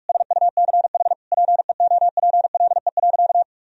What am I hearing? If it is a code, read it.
Morse code, heard as a keyed tone, about 46 wpm, 700 Hz. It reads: SUCH PEOPLE.
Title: Sentences from Top 1000 Words - Review Part 3 - 45wpm